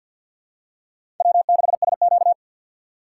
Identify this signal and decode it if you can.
Morse code, heard as a keyed tone, about 48 wpm, 690 Hz. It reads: W6SQ
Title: USA Call Signs - 4 Characters - 6x Character Spacing - 50wpm